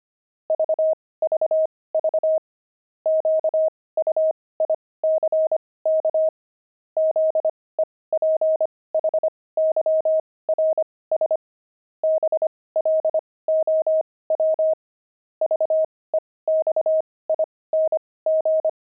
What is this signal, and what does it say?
Morse code, heard as a keyed tone, about 25 wpm, spaced as for 18 wpm, 630 Hz. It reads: VVV QUICK ZEPHYRS BLOW VEXING
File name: pangram-fs-18-cs-25-zebras